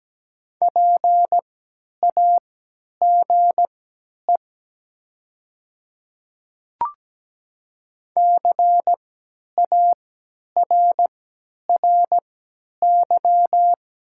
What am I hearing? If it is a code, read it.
Morse code, heard as a keyed tone, about 17 wpm, spaced as for 10 wpm, 700 Hz. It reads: PAGE CARRY
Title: 3x Character Spacing - Top 300 Words - 17wpm